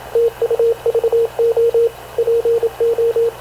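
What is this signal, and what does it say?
Morse code, heard as a keyed tone, about 26 wpm, 460 Hz, with radio noise. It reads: TUVO PO
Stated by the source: Havana Radio / CLA sending Spanish press (PX) in Morse, recorded by Radio Officer Paul Dumesnil onboard CNAV QUEST/CZDO.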